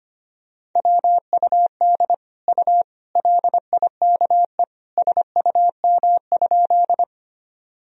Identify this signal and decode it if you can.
Morse code, heard as a keyed tone, about 25 wpm, 700 Hz. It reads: WUD U LIKE SUM?
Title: Sentences from Top 100 Words and Abbreviations 25wpm